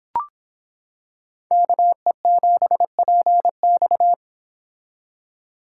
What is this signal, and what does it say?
Morse code, heard as a keyed tone, about 26 wpm, 700 Hz. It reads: KE7PX